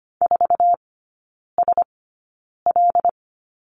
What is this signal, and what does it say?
Morse code, heard as a keyed tone, about 25 wpm, spaced as for 9 wpm, 700 Hz. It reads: 4SL
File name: Lesson 14e - Number 2 - Three Characters 25wpm